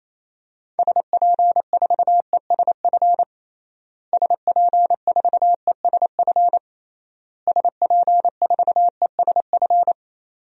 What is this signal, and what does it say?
Morse code, heard as a keyed tone, about 28 wpm, 700 Hz. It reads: SP4ESF SP4ESF SP4ESF
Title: DXCC-100 Call Signs - 6 Characters - Repeated 3x - 28wpm